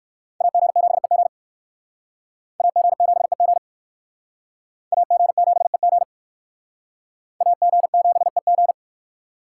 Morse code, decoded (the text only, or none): AG7EG AG7EG AG7EG AG7EG